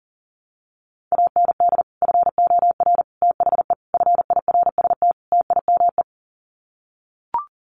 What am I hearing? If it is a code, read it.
Morse code, heard as a keyed tone, about 40 wpm, 700 Hz. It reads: AND FOR THE FIRST TIME
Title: Sets of 5 Words 40wpm